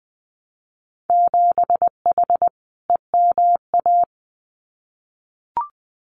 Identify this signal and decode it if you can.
Morse code, heard as a keyed tone, about 20 wpm, 700 Hz. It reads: 7H EMA
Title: Winter Field Day Exchanges Encore 20wpm